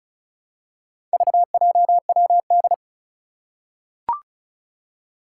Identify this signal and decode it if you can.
Morse code, heard as a keyed tone, about 36 wpm, 700 Hz. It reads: VJWD
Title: Four Challenging Letter-Number Combinations 35wpm